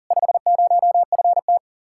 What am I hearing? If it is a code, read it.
Morse code, heard as a keyed tone, about 51 wpm, 700 Hz. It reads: 50FT